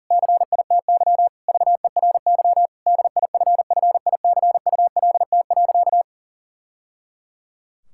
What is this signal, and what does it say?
Morse code, heard as a keyed tone, about 40 wpm, 700 Hz. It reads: CITY VERY DIFFICULT.
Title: Sentences from Top 1000 Words - Review Part 6 - 40wpm